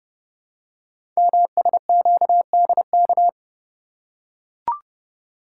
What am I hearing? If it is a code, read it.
Morse code, heard as a keyed tone, about 28 wpm, 700 Hz. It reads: MSQDK
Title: Code Groups - 5 Letters - 30wpm